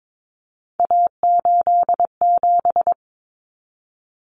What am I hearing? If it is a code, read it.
Morse code, heard as a keyed tone, about 22 wpm, 700 Hz. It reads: A87
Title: Common Cut Numbers - 3-Digit 22wpm